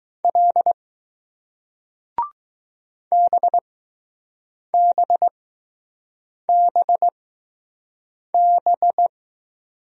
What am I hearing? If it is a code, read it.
Morse code, heard as a keyed tone, about 20 wpm, spaced as for 6 wpm, 700 Hz. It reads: L BBBB